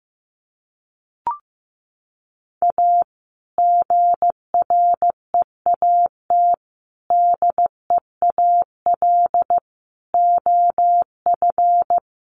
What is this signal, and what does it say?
Morse code, heard as a keyed tone, about 15 wpm, 700 Hz. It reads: A GREAT DEAL OF
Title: Sets of 4 Words 15wpm